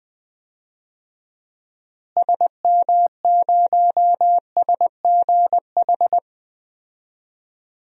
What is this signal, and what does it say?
Morse code, heard as a keyed tone, about 20 wpm, 700 Hz. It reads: SM0SGH
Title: DXCC-100 Call Signs - 6 Characters - Repeated 2x - 20wpm